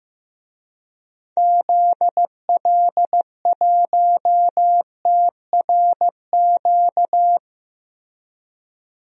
Morse code, heard as a keyed tone, about 15 wpm, 700 Hz. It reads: ZL1TRQ
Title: DXCC-100 Call Signs - 6 Characters - Repeated 2x - 15wpm